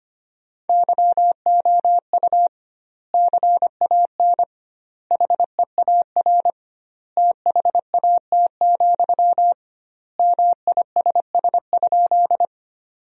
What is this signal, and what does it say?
Morse code, heard as a keyed tone, about 25 wpm, 700 Hz. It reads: YOU CAN HEAR THAT, MISS?
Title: 2x Word-Spacing - Sentences from Top 500 Words - 25wpm